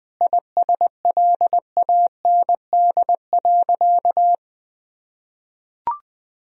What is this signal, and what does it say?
Morse code, heard as a keyed tone, about 20 wpm, 700 Hz. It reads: ISLAND.
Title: Sentences from Top 1000 Words - Review Part 3 - 20wpm